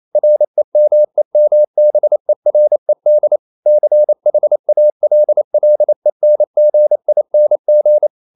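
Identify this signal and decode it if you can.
Morse code, heard as a keyed tone, about 28 wpm, 600 Hz. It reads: REMEMBERED CHALLENGING
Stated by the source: Morse code practice at 28 wpm, text of Ken Follett's novel Never in English, from CW med Gnister.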